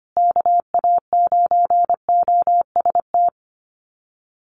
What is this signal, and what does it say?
Morse code, heard as a keyed tone, about 25 wpm, 700 Hz. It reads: KA9OST
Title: USA Call Signs Repeated 2x 25wpm